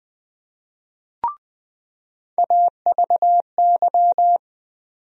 Morse code, heard as a keyed tone, about 20 wpm, 700 Hz. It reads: AVY
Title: USA Call Sign Postfixes - 3 Letters 20wpm